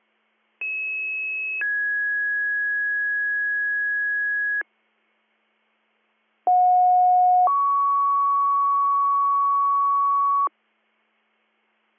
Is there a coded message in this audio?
Yes: two-tone page from 0.6 to 10.5 s, heard as a one-second tone, then a three-second tone, about 2600 then 1650 Hz, about 730 then 1120 Hz.